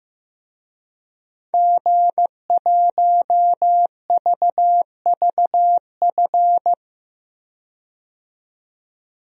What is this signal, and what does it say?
Morse code, heard as a keyed tone, about 15 wpm, 700 Hz. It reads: G1VVF
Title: DXCC-100 Call Signs - 5 Characters - 15wpm